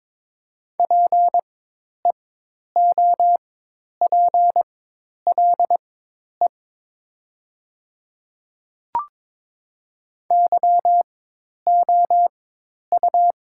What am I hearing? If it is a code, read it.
Morse code, heard as a keyed tone, about 22 wpm, spaced as for 10 wpm, 700 Hz. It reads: PEOPLE YOU
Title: 4x Character Spacing - Top 100 Words - 22wpm